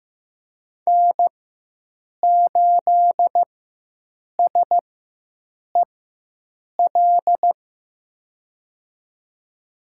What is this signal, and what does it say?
Morse code, heard as a keyed tone, about 15 wpm, spaced as for 7 wpm, 700 Hz. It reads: N8SEL
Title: USA Call Signs - 5 Characters - 4x Character Spacing - 15wpm